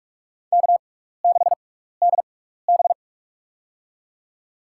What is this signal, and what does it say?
Morse code, heard as a keyed tone, about 47 wpm, spaced as for 16 wpm, 700 Hz. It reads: K6DB
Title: USA Call Signs - 4 Characters - 6x Character Spacing - 45wpm